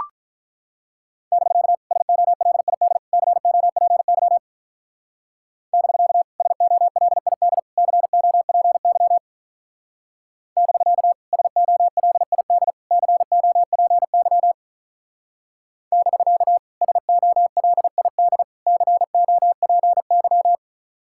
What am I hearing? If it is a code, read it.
Morse code, heard as a keyed tone, about 43 wpm, 700 Hz. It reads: <BK> SOLID COPY <BK> SOLID COPY <BK> SOLID COPY <BK> SOLID COPY
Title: Speed-Racing Common QSO Phrases - 35wpm